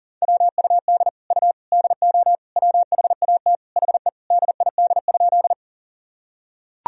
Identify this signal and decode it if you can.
Morse code, heard as a keyed tone, about 40 wpm, 700 Hz. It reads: WUD U DO WHAT HE DID?
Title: Sentences from Top 100 Words and Abbreviations 40wpm